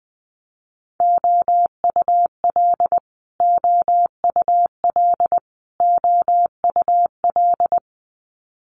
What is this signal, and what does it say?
Morse code, heard as a keyed tone, about 20 wpm, 700 Hz. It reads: OUL OUL OUL
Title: Multi-letter Phonemes Repeated 3x 20wpm